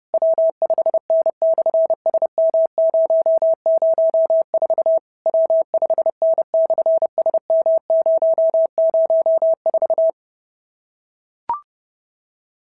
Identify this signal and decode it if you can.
Morse code, heard as a keyed tone, about 30 wpm, 650 Hz. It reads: W5N/SM004 W5N/SM004
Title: SOTA QSO Practice 30wpm